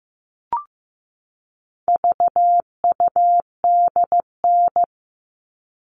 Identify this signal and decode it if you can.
Morse code, heard as a keyed tone, about 14 wpm, 700 Hz. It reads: VUDN
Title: All Cut Numbers - 4-Digit 15wpm